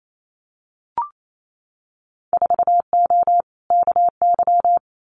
Morse code, heard as a keyed tone, about 28 wpm, 700 Hz. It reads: 4O KY
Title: Winter Field Day Exchanges Encore 28wpm